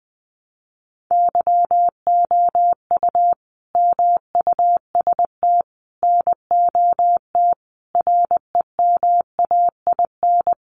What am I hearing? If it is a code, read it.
Morse code, heard as a keyed tone, about 20 wpm, 700 Hz. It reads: YOU MUST NOT REMAIN
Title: Sentences from Top 1200 Words 20wpm